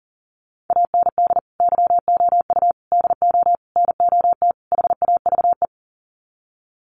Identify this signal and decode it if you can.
Morse code, heard as a keyed tone, about 40 wpm, 700 Hz. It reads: AND YOU DO NOT HAVE
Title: Sets of 5 Words Encore 40wpm